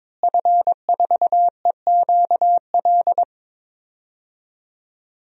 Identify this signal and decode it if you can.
Morse code, heard as a keyed tone, about 22 wpm, 700 Hz. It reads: F4EQL
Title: DXCC-100 Call Signs - 5 Characters - Repeated 3x - 22wpm